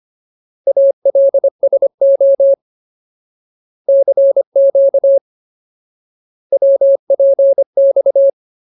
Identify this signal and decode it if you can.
Morse code, heard as a keyed tone, about 25 wpm, 550 Hz. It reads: ALSO CQ WPX
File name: Ultimate Word List / 25WPM Ultimate